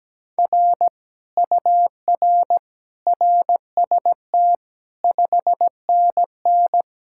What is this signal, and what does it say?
Morse code, heard as a keyed tone, about 17 wpm, 700 Hz. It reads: R UR RST 5NN
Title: Speed-Racing Common QSO Phrases - 17wpm